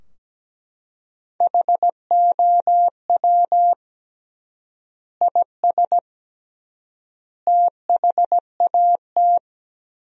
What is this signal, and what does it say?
Morse code, heard as a keyed tone, about 17 wpm, 700 Hz. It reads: HOW IS THAT
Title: 3x Word-Spacing - Sentences from Top 500 Words - 17wpm